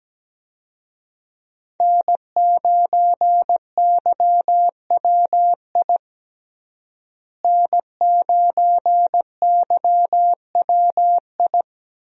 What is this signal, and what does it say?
Morse code, heard as a keyed tone, about 17 wpm, 700 Hz. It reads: N9YWI N9YWI
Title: USA Call Signs - 5 Characters - Repeated 3x - 17wpm